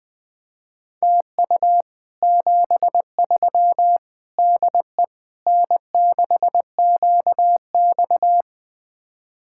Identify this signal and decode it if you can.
Morse code, heard as a keyed tone, about 21 wpm, 700 Hz. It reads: TU 73 DE N6QX
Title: Speed-Racing Common QSO Phrases Encore - 20wpm